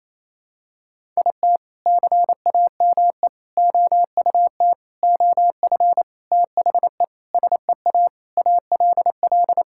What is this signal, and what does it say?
Morse code, heard as a keyed tone, about 28 wpm, 700 Hz. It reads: IT CAME OUT OF THE SEA ALL
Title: Sentences from Top 1000 Words - Review Part 5 - 28wpm